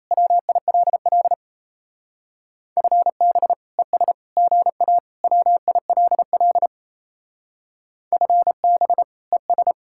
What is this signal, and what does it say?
Morse code, heard as a keyed tone, about 34 wpm, 700 Hz. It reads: WILL FB ES GA WILL FB ES
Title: Speed-Racing Common QSO Phrases - 28wpm